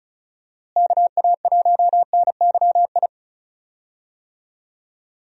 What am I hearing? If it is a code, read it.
Morse code, heard as a keyed tone, about 34 wpm, 700 Hz. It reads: KA1NYI
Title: USA Call Signs - 6 Characters - Repeated 2x - 35wpm